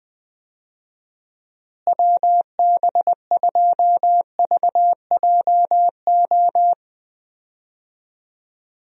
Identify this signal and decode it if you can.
Morse code, heard as a keyed tone, about 21 wpm, 700 Hz. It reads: WB2VJO
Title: USA Call Signs - 6 Characters - Repeated 2x - 20wpm